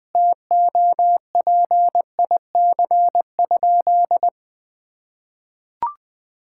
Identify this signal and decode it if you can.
Morse code, heard as a keyed tone, about 20 wpm, 700 Hz. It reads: TOPIC?
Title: Sentences with 8-Word Phrases - 20wpm